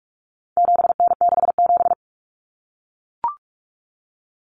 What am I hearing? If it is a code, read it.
Morse code, heard as a keyed tone, about 44 wpm, 700 Hz. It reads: 7N67
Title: Common Cut Numbers - 4-Digit 45wpm